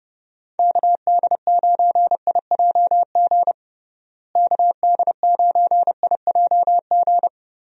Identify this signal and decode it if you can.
Morse code, heard as a keyed tone, about 30 wpm, 700 Hz. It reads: KD9IJG KD9IJG
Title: USA Call Signs - 6 Characters - Repeated 3x - 30wpm